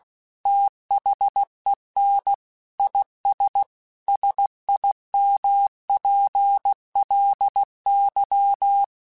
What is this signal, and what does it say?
Morse code, heard as a keyed tone, about 16 wpm, 800 Hz. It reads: THEN IS SIMPLY
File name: Technical Subjects / Technical Subjects-mp3s / XL @15